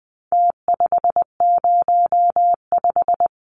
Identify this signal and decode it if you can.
Morse code, heard as a keyed tone, about 20 wpm, 700 Hz. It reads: T505
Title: Numbers - Contest Style - 4-Digit 20wpm